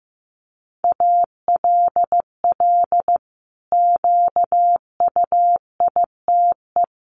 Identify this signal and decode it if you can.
Morse code, heard as a keyed tone, about 15 wpm, 700 Hz. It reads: ALL QUITE
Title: Sentences from Top 700 Words 15wpm